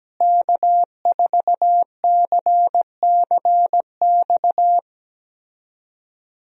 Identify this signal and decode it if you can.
Morse code, heard as a keyed tone, about 17 wpm, 700 Hz. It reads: K4CCX
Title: USA Call Signs - 5 Characters - Repeated 3x - 17wpm